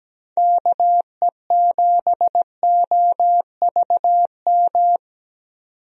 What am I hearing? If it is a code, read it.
Morse code, heard as a keyed tone, about 17 wpm, 700 Hz. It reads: KE7OVM
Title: USA Call Signs - 6 Characters - 17wpm